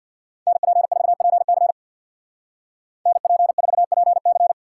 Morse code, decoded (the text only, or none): NP4PC NP4PC